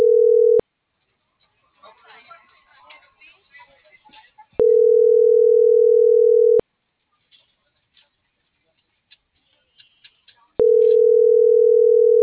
Ringback tone (North America), 440 + 480 Hz.